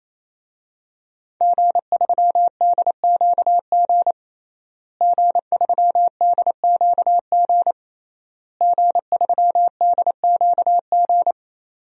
Morse code, heard as a keyed tone, about 28 wpm, 700 Hz. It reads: G3DQG G3DQG G3DQG
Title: DXCC-100 Call Signs - 5 Characters - Repeated 3x - 28wpm